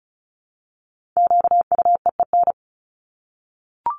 Morse code, heard as a keyed tone, about 34 wpm, 700 Hz. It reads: QUEEN
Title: Top 1751-2000 Words 35wpm